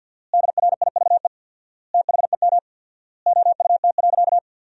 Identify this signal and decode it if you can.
Morse code, heard as a keyed tone, about 53 wpm, 700 Hz. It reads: DRIVE THEM OUT.
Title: 4x Word-Spacing - Sentences from Top 500 Words - 50wpm